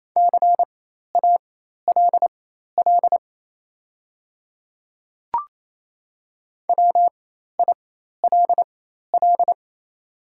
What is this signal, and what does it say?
Morse code, heard as a keyed tone, about 27 wpm, spaced as for 13 wpm, 690 Hz. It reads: CALL WILL
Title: 4x Character Spacing - Top 100 Words - 28wpm